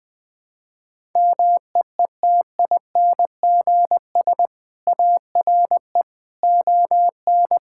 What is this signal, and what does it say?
Morse code, heard as a keyed tone, about 21 wpm, 700 Hz. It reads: MEETINGS ARE ON
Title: Easy Sentences with Hints - 20wpm